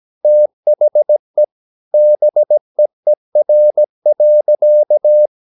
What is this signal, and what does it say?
Morse code, heard as a keyed tone, about 17 wpm, 600 Hz. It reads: THE BEER.
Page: Romanen Never af Ken Follet på engelsk.